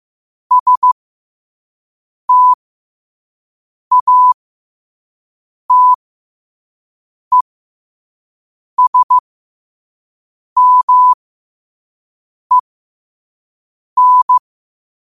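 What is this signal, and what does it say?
Morse code, heard as a keyed tone, about 15 wpm, spaced as for 5 wpm, 1000 Hz. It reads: STATESMEN
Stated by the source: Morse code quote of the day at 5 WPM.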